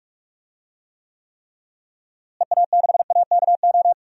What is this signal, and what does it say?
Morse code, heard as a keyed tone, about 42 wpm, 700 Hz. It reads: EA6AKO
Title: DXCC-100 Call Signs - 6 Characters - Repeated 3x - 45wpm